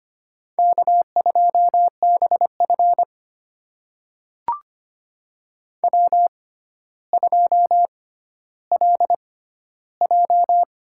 Morse code, heard as a keyed tone, about 25 wpm, spaced as for 13 wpm, 700 Hz. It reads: K2BF W2LJ
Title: USA Call Signs - 4 Characters - 6x Character Spacing - 25wpm